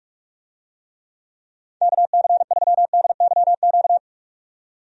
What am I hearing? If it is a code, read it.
Morse code, heard as a keyed tone, about 44 wpm, 700 Hz. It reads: KC3DYQ
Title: USA Call Signs - 6 Characters - 45wpm